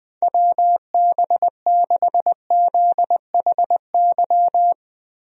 Morse code, heard as a keyed tone, about 20 wpm, 700 Hz. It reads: WB6ZHY
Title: USA Call Signs - 6 Characters - Repeated 3x - 20wpm